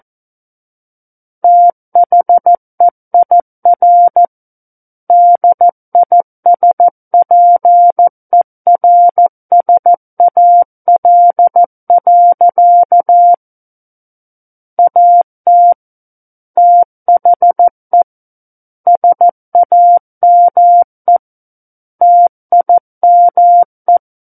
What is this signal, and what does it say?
Morse code, heard as a keyed tone, about 14 wpm, 700 Hz. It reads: THEIR DISPERSAL. AT THE SAME TIME
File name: Book One / 05-Chapter 5 - 14 WPM